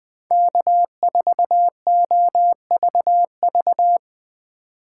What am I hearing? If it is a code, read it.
Morse code, heard as a keyed tone, about 21 wpm, 700 Hz. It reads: K4OVV
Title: Speed Racing - USA 1x3 Call Signs 17wpm